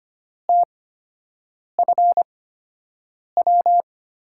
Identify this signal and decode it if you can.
Morse code, heard as a keyed tone, about 25 wpm, spaced as for 7 wpm, 700 Hz. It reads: TFW
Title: USA Call Signs - 5 Characters - 8x Character Spacing - 25wpm